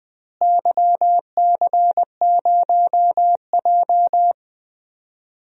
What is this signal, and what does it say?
Morse code, heard as a keyed tone, about 20 wpm, 700 Hz. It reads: YC0J
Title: DXCC-100 Call Signs - 4 Characters - Repeated 3x - 20wpm